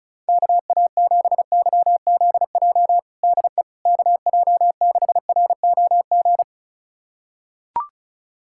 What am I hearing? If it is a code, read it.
Morse code, heard as a keyed tone, about 34 wpm, 700 Hz. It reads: KA7YZJ DE KJ6ROG
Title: USA Callsign DE Callsign - 6 Characters - 35wpm